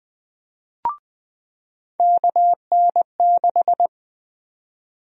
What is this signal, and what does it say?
Morse code, heard as a keyed tone, about 21 wpm, 700 Hz. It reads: KN6
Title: USA Call Sign Prefixes - 20wpm